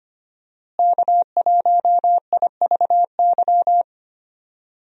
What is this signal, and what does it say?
Morse code, heard as a keyed tone, about 25 wpm, 700 Hz. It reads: K1IVY
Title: USA Call Signs - 5 Characters - Repeated 2x - 25wpm